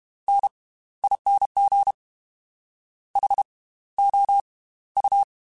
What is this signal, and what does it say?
Morse code, heard as a keyed tone, about 32 wpm, spaced as for 17 wpm, 800 Hz. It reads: NING HOU